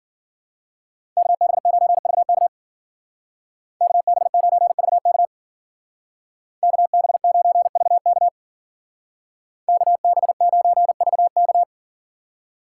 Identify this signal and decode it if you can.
Morse code, heard as a keyed tone, about 49 wpm, 700 Hz. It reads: KB9VK KB9VK KB9VK KB9VK
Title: Speed Racing - USA 2x2 Call Signs 40wpm